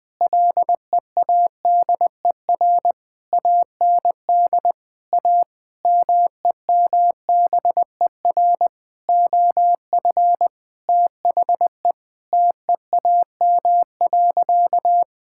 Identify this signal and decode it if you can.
Morse code, heard as a keyed tone, about 20 wpm, 700 Hz. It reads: LEADER AND A MEMBER OF THE TEAM.
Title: Sentences with 5-Word Phrases - Spoken Beginning - 20wpm